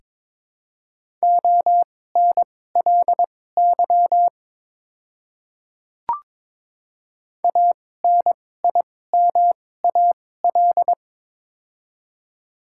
Morse code, heard as a keyed tone, about 23 wpm, spaced as for 16 wpm, 690 Hz. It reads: ONLY ANIMAL
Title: 2x Character Spacing - Top 200 Words - 22wpm